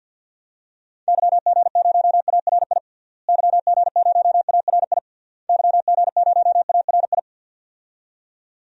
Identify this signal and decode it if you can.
Morse code, heard as a keyed tone, about 49 wpm, 700 Hz. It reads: YG0ARI YG0ARI YG0ARI
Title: DXCC-100 Call Signs - 6 Characters - Repeated 3x - 50wpm